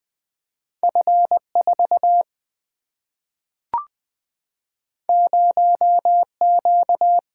Morse code, heard as a keyed tone, about 20 wpm, 700 Hz. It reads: F4 0Q
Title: Single-Letter Single-Number 20wpm